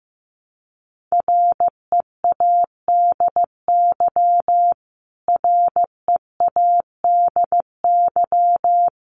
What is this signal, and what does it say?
Morse code, heard as a keyed tone, about 15 wpm, 700 Hz. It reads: READY READY
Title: Top 500 Words Repeated 15wpm